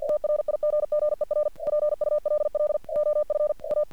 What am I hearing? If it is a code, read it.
Morse code, heard as a keyed tone, about 48 wpm, 600 Hz. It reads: TRIGGER PULL OF A
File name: archived / CRSnet